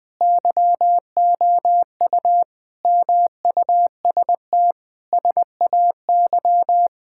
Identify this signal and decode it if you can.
Morse code, heard as a keyed tone, about 20 wpm, 700 Hz. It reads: YOU MUST SAY
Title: Sentences from Top 1600 Words 20wpm